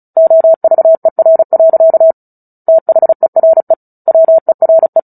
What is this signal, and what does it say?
Morse code, heard as a keyed tone, about 34 wpm, 650 Hz. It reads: OVER. THERE WERE
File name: Book Two / 09-Chapter 9 - 35 WPM